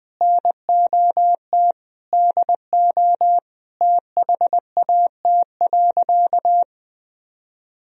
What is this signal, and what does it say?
Morse code, heard as a keyed tone, about 20 wpm, 700 Hz. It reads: NOT DO THAT.
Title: Sentences from Top 1000 Words - Review Part 4 - 20wpm